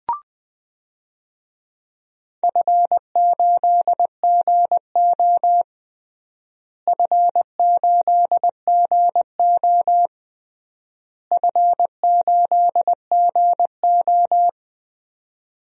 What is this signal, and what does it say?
Morse code, heard as a keyed tone, about 20 wpm, 690 Hz. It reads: F8GO F8GO F8GO
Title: DXCC-100 Call Signs - 4 Characters - Repeated 3x - 20wpm